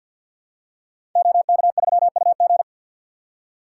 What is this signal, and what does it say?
Morse code, heard as a keyed tone, about 47 wpm, 700 Hz. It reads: OK3UG
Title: Code Groups - 5 Letter:Numbers - 50wpm